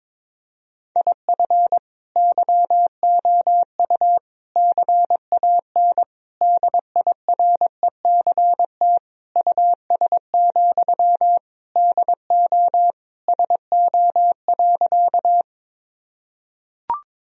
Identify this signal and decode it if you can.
Morse code, heard as a keyed tone, about 22 wpm, 700 Hz. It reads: IF YOU CAN DIRECT US, DO SO.
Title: Sentences from Top 1000 Words - Review Part 2 - 22wpm